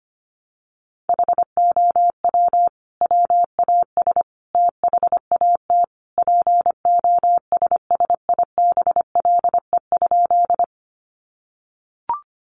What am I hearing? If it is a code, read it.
Morse code, heard as a keyed tone, about 25 wpm, 700 Hz. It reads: HOW WAS THAT POSSIBLE?
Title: Sentences from Top 500 Words 25wpm